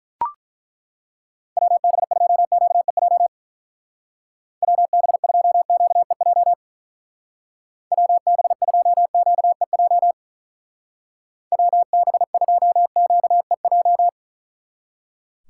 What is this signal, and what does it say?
Morse code, heard as a keyed tone, about 43 wpm, 700 Hz. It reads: WB2QEJ WB2QEJ WB2QEJ WB2QEJ
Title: Speed Racing - USA 2x3 Call Signs 35wpm